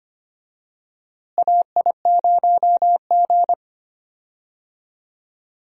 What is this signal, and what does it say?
Morse code, heard as a keyed tone, about 25 wpm, 700 Hz. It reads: AI0G
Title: USA Call Signs - 4 Characters - Repeated 3x - 25wpm